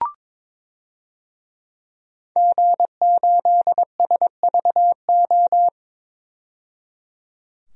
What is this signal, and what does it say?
Morse code, heard as a keyed tone, about 22 wpm, 700 Hz. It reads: G8SVO
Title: DXCC-100 Call Signs - 5 Characters - 22wpm